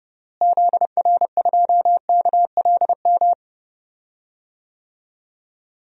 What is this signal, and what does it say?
Morse code, heard as a keyed tone, about 30 wpm, 700 Hz. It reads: ZR2KLM
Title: DXCC-100 Call Signs - 6 Characters - Repeated 2x - 30wpm